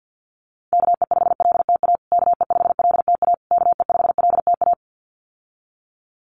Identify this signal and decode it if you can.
Morse code, heard as a keyed tone, about 50 wpm, 700 Hz. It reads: KE5LTA KE5LTA KE5LTA
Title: USA Call Signs Repeated 3x 50wpm